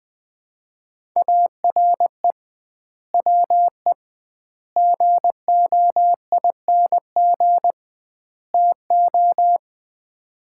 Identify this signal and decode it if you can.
Morse code, heard as a keyed tone, about 20 wpm, 700 Hz. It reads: ARE WE GOING TO
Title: 2x Word-Spacing - Sets of 4 Words - 20wpm